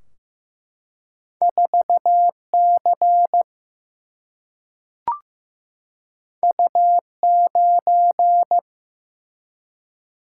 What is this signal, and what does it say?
Morse code, heard as a keyed tone, about 15 wpm, 700 Hz. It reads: 4 C U 9
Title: Single-Letter Single-Number 15wpm